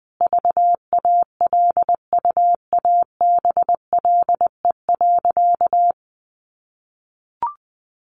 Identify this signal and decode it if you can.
Morse code, heard as a keyed tone, about 20 wpm, 700 Hz. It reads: VALUABLE.
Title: Sentences from Top 1200 Words 20wpm